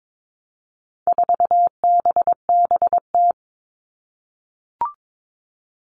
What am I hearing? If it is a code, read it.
Morse code, heard as a keyed tone, about 22 wpm, 700 Hz. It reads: 4BBT
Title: All Cut Numbers - 4-Digit 22wpm